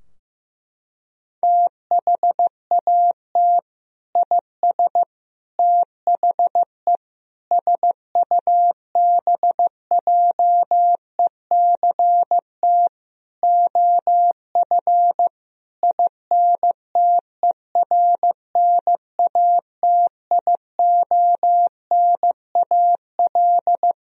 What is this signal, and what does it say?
Morse code, heard as a keyed tone, about 15 wpm, 700 Hz. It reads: THAT IS THE SUBJECT OF INTERNATIONAL
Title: Sentences from Top 1000 Words - Review Part 4 - 15wpm